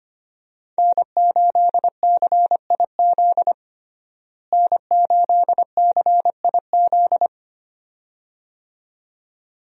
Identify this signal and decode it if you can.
Morse code, heard as a keyed tone, about 25 wpm, 700 Hz. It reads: N8CIZ N8CIZ
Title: USA Call Signs - 5 Characters - Repeated 2x - 25wpm